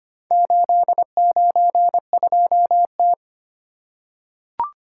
Morse code, heard as a keyed tone, about 26 wpm, 700 Hz. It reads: 892T
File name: practice / 25wpm / Four Letter-Number Combinations - 25wpm